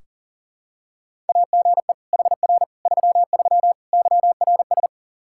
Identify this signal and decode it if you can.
Morse code, heard as a keyed tone, about 40 wpm, 700 Hz. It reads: AGE HR 33 YRS
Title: Practice QSO - Exchange 3 - More Info - 40wpm